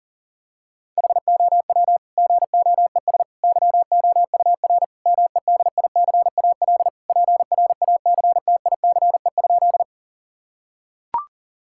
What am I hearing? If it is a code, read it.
Morse code, heard as a keyed tone, about 40 wpm, 700 Hz. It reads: HOW GOES YOUR MEDICAL PRACTICE?
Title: Sentences from Top 1000 Words - Review Part 5 - 40wpm